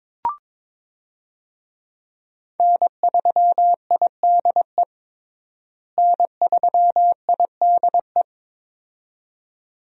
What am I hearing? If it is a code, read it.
Morse code, heard as a keyed tone, about 22 wpm, 700 Hz. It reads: N3IDE N3IDE
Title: USA Call Signs - 5 Characters - Repeated 2x - 22wpm